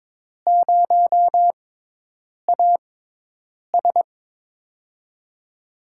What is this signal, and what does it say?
Morse code, heard as a keyed tone, about 22 wpm, spaced as for 8 wpm, 700 Hz. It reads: 0AS
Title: USA Call Signs - 5 Characters - 6x Character Spacing - 22wpm